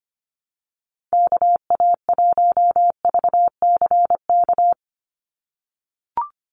Morse code, heard as a keyed tone, about 25 wpm, 700 Hz. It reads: KA1VCK
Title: USA Call Signs 25wpm